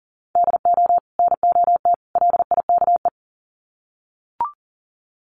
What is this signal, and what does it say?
Morse code, heard as a keyed tone, about 37 wpm, 700 Hz. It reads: DO NOT LIKE
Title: Sets of 3 Words 40wpm